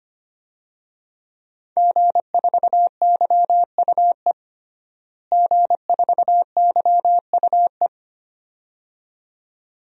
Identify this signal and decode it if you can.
Morse code, heard as a keyed tone, about 25 wpm, 700 Hz. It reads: G4YUE G4YUE
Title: DXCC-100 Call Signs - 5 Characters - Repeated 2x - 25wpm